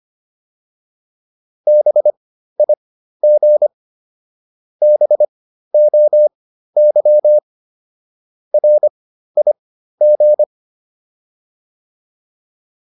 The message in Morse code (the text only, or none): BIG BOY RIG